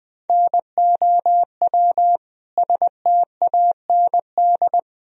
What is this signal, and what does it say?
Morse code, heard as a keyed tone, about 20 wpm, 700 Hz. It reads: NOW STAND
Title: Sentences from Top 1000 Words - Review Part 1 - 20wpm